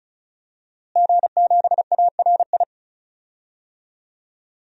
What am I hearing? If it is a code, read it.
Morse code, heard as a keyed tone, about 36 wpm, 700 Hz. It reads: G7ARI
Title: DXCC-100 Call Signs - 5 Characters - Repeated 3x - 35wpm